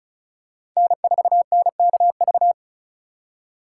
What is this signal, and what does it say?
Morse code, heard as a keyed tone, about 34 wpm, 700 Hz. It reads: N4NKV
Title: USA Call Signs - 5 Characters - 4x Character Spacing - 35wpm